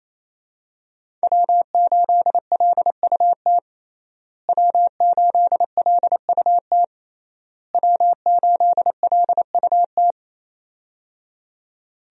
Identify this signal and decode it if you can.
Morse code, heard as a keyed tone, about 28 wpm, 700 Hz. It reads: W8LUT W8LUT W8LUT
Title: USA Call Signs - 5 Characters - Repeated 3x - 28wpm